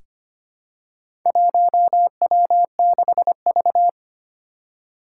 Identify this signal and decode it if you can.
Morse code, heard as a keyed tone, about 25 wpm, 700 Hz. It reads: 1W6V
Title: Four Challenging Letter-Number Combinations 25wpm